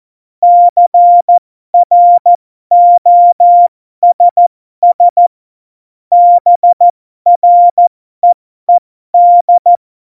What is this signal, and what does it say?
Morse code, heard as a keyed tone, about 14 wpm, spaced as for 12 wpm, 700 Hz. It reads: CROSS BREED
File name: Morsekurs / Witze